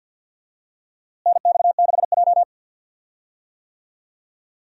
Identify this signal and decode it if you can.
Morse code, heard as a keyed tone, about 52 wpm, 700 Hz. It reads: NX6J
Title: USA Call Signs - 4 Characters - Repeated 3x - 50wpm